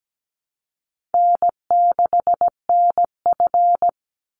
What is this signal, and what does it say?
Morse code, heard as a keyed tone, about 17 wpm, 700 Hz. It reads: N6NF
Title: Easy Call Signs Repeated 2x - 17wpm